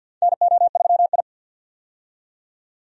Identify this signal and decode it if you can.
Morse code, heard as a keyed tone, about 46 wpm, 700 Hz. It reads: NO3I
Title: USA Call Signs - 4 Characters - 50wpm